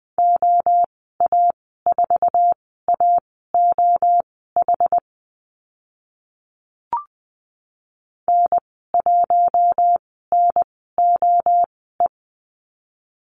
Morse code, heard as a keyed tone, about 20 wpm, spaced as for 15 wpm, 700 Hz. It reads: OA4AOH N1NOE
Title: 069 Lesson 12g - Letter d - Call Signs 20wpm